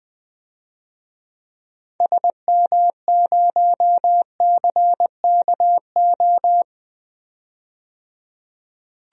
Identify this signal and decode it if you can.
Morse code, heard as a keyed tone, about 20 wpm, 700 Hz. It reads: SM0CKO
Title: DXCC-100 Call Signs - 6 Characters - 20wpm